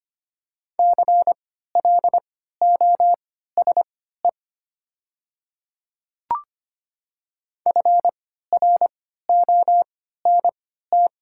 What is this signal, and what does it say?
Morse code, heard as a keyed tone, about 25 wpm, spaced as for 14 wpm, 700 Hz. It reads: CLOSE FRONT
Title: 3x Character Spacing - Top 500 Words - 25wpm